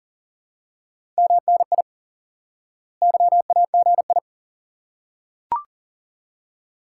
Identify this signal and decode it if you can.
Morse code, heard as a keyed tone, about 40 wpm, 700 Hz. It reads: MNI YAGI
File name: IQE Course 3 - Lesson 89 - YAGI - Two Elements - 40wpm